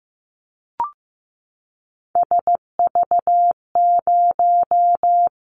Morse code, heard as a keyed tone, about 15 wpm, 700 Hz. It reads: SV0
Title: Callsign Prefixes for Top 50 DXCC - 15wpm